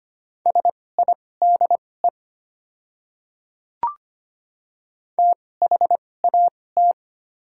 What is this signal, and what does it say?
Morse code, heard as a keyed tone, about 25 wpm, spaced as for 18 wpm, 700 Hz. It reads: SIDE THAT
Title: 2x Character Spacing - Top 200 Words - 25wpm